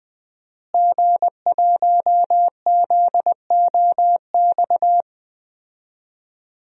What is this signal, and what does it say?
Morse code, heard as a keyed tone, about 20 wpm, 700 Hz. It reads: G1ZOX
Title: DXCC-100 Call Signs - 5 Characters - Repeated 3x - 20wpm